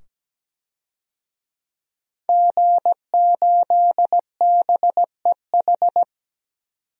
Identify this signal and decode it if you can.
Morse code, heard as a keyed tone, about 17 wpm, 700 Hz. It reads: G8BEH
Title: DXCC-100 Call Signs - 5 Characters - 17wpm